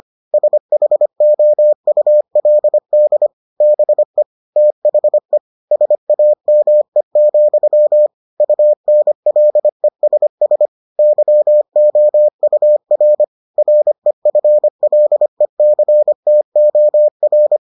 Morse code, heard as a keyed tone, about 25 wpm, 600 Hz. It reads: SHOULD BE THE SAME, UNLESS YOUR REFLECTOR